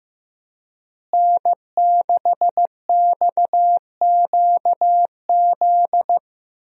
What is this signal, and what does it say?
Morse code, heard as a keyed tone, about 15 wpm, 700 Hz. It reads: N6XQZ
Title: USA Call Signs - 5 Characters - Repeated 2x - 15wpm